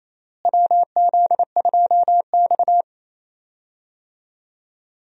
Morse code, heard as a keyed tone, about 28 wpm, 700 Hz. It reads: WZ2X
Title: USA Call Signs - 4 Characters - Repeated 2x - 28wpm